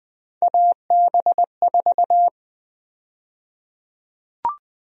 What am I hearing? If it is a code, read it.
Morse code, heard as a keyed tone, about 20 wpm, 700 Hz. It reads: AB4
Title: USA Call Sign Prefixes - 20wpm